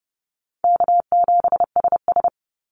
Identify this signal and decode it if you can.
Morse code, heard as a keyed tone, about 30 wpm, 700 Hz. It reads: K7SS
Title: Easy Call Signs - 30wpm